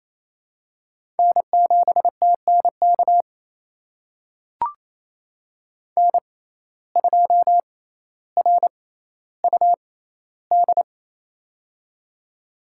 Morse code, heard as a keyed tone, about 28 wpm, spaced as for 14 wpm, 700 Hz. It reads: N7TNK N2RUD
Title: USA Call Signs - 5 Characters - 6x Character Spacing - 28wpm